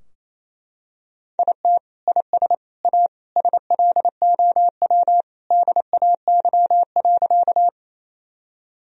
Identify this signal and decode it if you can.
Morse code, heard as a keyed tone, about 28 wpm, 700 Hz. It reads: IT IS A SLOW DAY.
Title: Sentences from Top 1000 Words - Review Part 2 - 28wpm